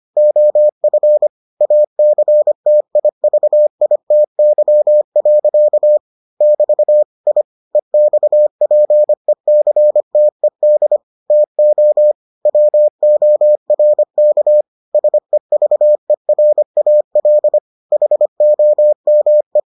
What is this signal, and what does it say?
Morse code, heard as a keyed tone, about 25 wpm, 600 Hz. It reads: OF ACTIVITY. <BT> I EXPECTED TO WORK SEVERAL HOME